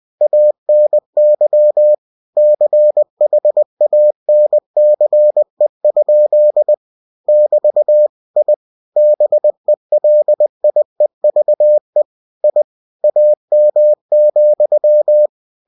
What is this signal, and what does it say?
Morse code, heard as a keyed tone, about 20 wpm, 600 Hz. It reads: ANY CHANCE? <BT> I BELIEVE I AM,
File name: cw-books / winnie-the-pooh / winnie-20wpm-page-0084